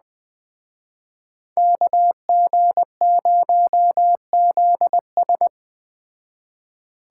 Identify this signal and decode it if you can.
Morse code, heard as a keyed tone, about 20 wpm, 700 Hz. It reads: KG0ZS
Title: Speed Racing - USA Call Signs 17wpm